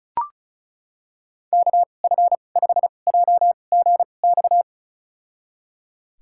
Morse code, heard as a keyed tone, about 36 wpm, spaced as for 25 wpm, 700 Hz. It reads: KF5JGX
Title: USA Call Signs - 6 Characters - 2x Character Spacing - 35wpm